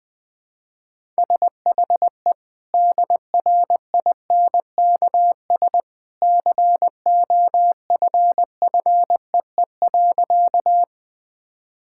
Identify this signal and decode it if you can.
Morse code, heard as a keyed tone, about 21 wpm, 700 Hz. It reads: SHE DRINKS COFFEE.
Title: Easy Sentences with Hints - 20wpm